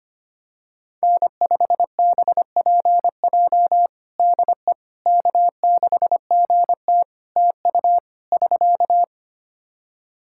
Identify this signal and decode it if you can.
Morse code, heard as a keyed tone, about 25 wpm, 700 Hz. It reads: N5BPJ DE K6GT TU <SK>
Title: K1USN SST QSO Practice 25wpm